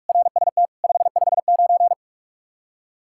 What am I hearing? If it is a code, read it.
Morse code, heard as a keyed tone, about 47 wpm, 700 Hz. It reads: RST 559
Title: Practice QSO - Exchange 1 - Beginning - 45wpm